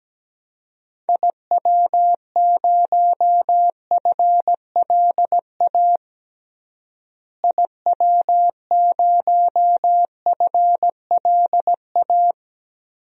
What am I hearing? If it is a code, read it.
Morse code, heard as a keyed tone, about 17 wpm, 700 Hz. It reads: IW0FLA IW0FLA
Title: DXCC-100 Call Signs - 6 Characters - Repeated 3x - 17wpm